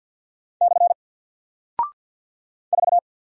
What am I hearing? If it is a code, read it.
Morse code, heard as a keyed tone, about 48 wpm, 700 Hz. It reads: / 4
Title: Single Character (Letter:Number:Punctuation:Prosign) Alt 50wpm